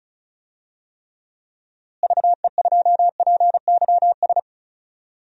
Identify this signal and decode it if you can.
Morse code, heard as a keyed tone, about 36 wpm, 700 Hz. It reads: VE2PYS
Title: DXCC-100 Call Signs - 6 Characters - Repeated 3x - 35wpm